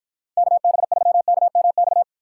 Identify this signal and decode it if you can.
Morse code, heard as a keyed tone, about 54 wpm, 700 Hz. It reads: KB3KMX